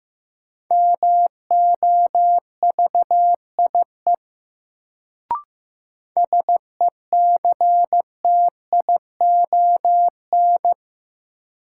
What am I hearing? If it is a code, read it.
Morse code, heard as a keyed tone, about 15 wpm, 700 Hz. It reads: MOVIE SECTION
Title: Sentences from Top 1000 Words - Review Part 5 - New Word Drill - 15wpm